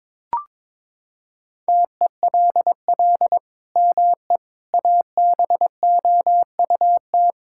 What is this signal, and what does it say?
Morse code, heard as a keyed tone, about 22 wpm, 700 Hz. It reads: TELL ME ABOUT
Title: Sentences from Top 1000 Words - Review Part 5 - 22wpm